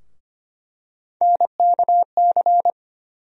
Morse code, heard as a keyed tone, about 25 wpm, 700 Hz. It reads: NKC
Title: USA Call Sign Postfixes - 3 Letters 25wpm